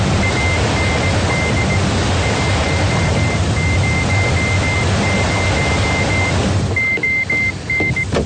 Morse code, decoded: AND 100 ON